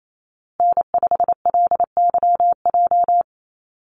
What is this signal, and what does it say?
Morse code, heard as a keyed tone, about 28 wpm, 700 Hz. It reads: N5LYJ
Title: Call Signs Repeated 2x - 28wpm